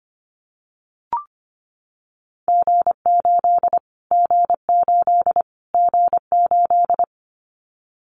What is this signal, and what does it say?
Morse code, heard as a keyed tone, about 25 wpm, 700 Hz. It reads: G8 G8 G8
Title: Callsign Prefixes for Top 25 DXCC - Repeated 3x - 25wpm